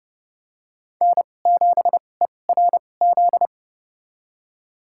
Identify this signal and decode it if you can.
Morse code, heard as a keyed tone, about 29 wpm, spaced as for 21 wpm, 700 Hz. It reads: N7ERZ
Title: USA Call Signs - 5 Characters - 2x Character Spacing - 30wpm